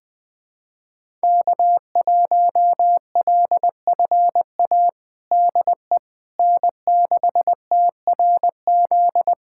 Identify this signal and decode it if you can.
Morse code, heard as a keyed tone, about 20 wpm, 700 Hz. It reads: K1LFA DE N6TRZ
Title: USA Callsign DE Callsign - 5 Characters - 20wpm